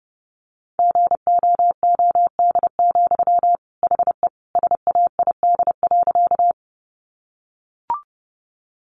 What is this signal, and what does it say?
Morse code, heard as a keyed tone, about 30 wpm, 700 Hz. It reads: GOOD, HE SAID.
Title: Sentences from Top 200 Words 30wpm